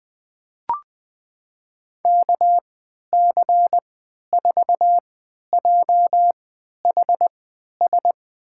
Morse code, heard as a keyed tone, about 20 wpm, spaced as for 11 wpm, 700 Hz. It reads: KC4JHS
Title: USA Call Signs - 6 Characters - 3x Character Spacing - 20wpm